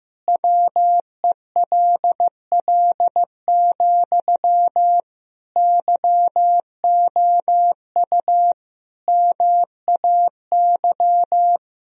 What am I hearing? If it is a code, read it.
Morse code, heard as a keyed tone, about 15 wpm, 700 Hz. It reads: WELL, YOU MAY
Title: Sentences from Top 1000 Words - Review Part 4 - 15wpm